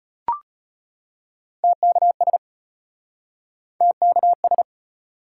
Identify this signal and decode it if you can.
Morse code, heard as a keyed tone, about 34 wpm, 700 Hz. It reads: TKS TKS